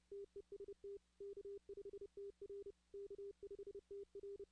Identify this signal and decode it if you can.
Morse code, heard as a keyed tone, about 30 wpm, 400 Hz. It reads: TEST K5TR K5TR
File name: 2016 arrl dx cw / 2016-ARRL-DX-CW-K5TR- - 0-46